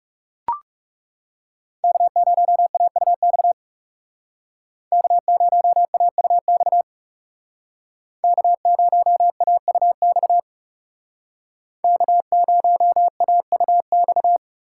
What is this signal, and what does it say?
Morse code, heard as a keyed tone, about 39 wpm, 700 Hz. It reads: K0AUX K0AUX K0AUX K0AUX